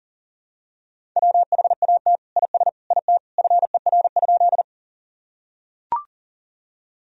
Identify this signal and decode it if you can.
Morse code, heard as a keyed tone, about 40 wpm, 700 Hz. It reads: WHAT IS IT FER?
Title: Sentences from Top 100 Words and Abbreviations 40wpm